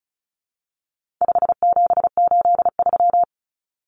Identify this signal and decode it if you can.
Morse code, heard as a keyed tone, about 34 wpm, 700 Hz. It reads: 5783
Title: Numbers - 4-Digit 35wpm